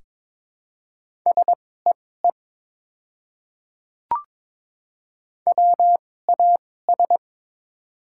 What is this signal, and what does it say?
Morse code, heard as a keyed tone, about 22 wpm, spaced as for 16 wpm, 700 Hz. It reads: SEE WAS
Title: 2x Character Spacing - Top 100 Words - 22wpm